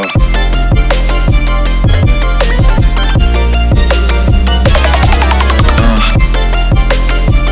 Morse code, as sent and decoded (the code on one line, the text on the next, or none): none